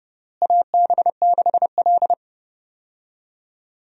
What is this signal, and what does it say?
Morse code, heard as a keyed tone, about 32 wpm, 700 Hz. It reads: AB6L
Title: USA Call Signs - 4 Characters - Repeated 3x - 30wpm